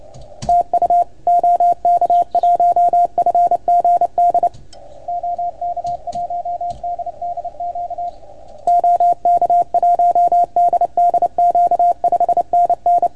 Morse code, heard as a keyed tone, about 30 wpm, 690 Hz, with radio noise. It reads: TU OK1FGD OK1DDQ OK1DDQ 5NN
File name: Archiv / CUC310 / OK1FGD CUC 310 pileup